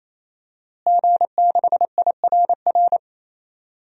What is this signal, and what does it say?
Morse code, heard as a keyed tone, about 27 wpm, 700 Hz. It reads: G6IRR
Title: DXCC-100 Call Signs - 5 Characters - Repeated 3x - 28wpm